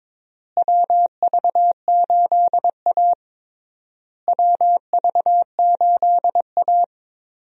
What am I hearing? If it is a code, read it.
Morse code, heard as a keyed tone, about 22 wpm, 700 Hz. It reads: WV8A WV8A
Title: USA Call Signs - 4 Characters - Repeated 3x - 22wpm